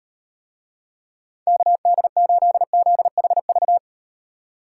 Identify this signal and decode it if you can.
Morse code, heard as a keyed tone, about 35 wpm, 700 Hz. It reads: KD8ZHV